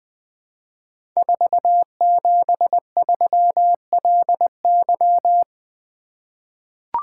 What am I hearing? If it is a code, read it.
Morse code, heard as a keyed tone, about 20 wpm, 700 Hz. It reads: 473LY
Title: Code Groups - 5 Letter:Numbers - 20wpm